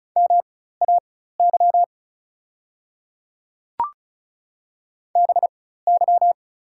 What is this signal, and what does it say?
Morse code, heard as a keyed tone, about 34 wpm, spaced as for 16 wpm, 700 Hz. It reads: MAY BY
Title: 4x Character Spacing - Top 200 Words - 35wpm